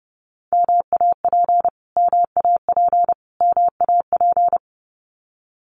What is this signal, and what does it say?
Morse code, heard as a keyed tone, about 30 wpm, 700 Hz. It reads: MAP MAP MAP
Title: Top 500 Words Repeated 30wpm